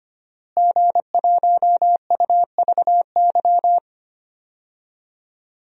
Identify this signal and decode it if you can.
Morse code, heard as a keyed tone, about 25 wpm, 700 Hz. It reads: G1UVY
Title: DXCC-100 Call Signs - 5 Characters - Repeated 2x - 25wpm